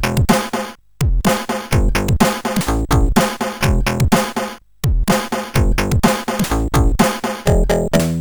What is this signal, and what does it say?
There is no code.